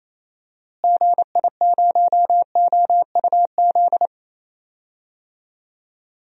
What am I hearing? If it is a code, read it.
Morse code, heard as a keyed tone, about 28 wpm, 700 Hz. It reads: GI0OUZ
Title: DXCC-100 Call Signs - 6 Characters - Repeated 3x - 28wpm